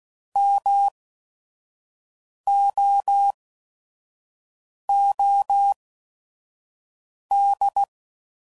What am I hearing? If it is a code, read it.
Morse code, heard as a keyed tone, about 16 wpm, spaced as for 5 wpm, 800 Hz. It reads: MOOD